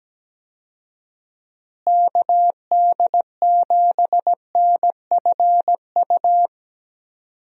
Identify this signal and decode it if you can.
Morse code, heard as a keyed tone, about 17 wpm, 700 Hz. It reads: KD7NFU
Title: USA Call Signs - 6 Characters - Repeated 2x - 17wpm